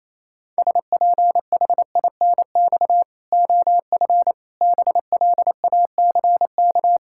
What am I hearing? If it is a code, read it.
Morse code, heard as a keyed tone, about 28 wpm, 700 Hz. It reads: SPHINX OF BLACK
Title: Pangram Challenge v2 - 28wpm